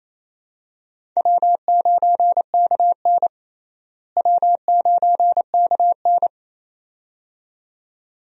Morse code, heard as a keyed tone, about 28 wpm, 700 Hz. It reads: W9KN W9KN
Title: USA Call Signs - 4 Characters - Repeated 2x - 28wpm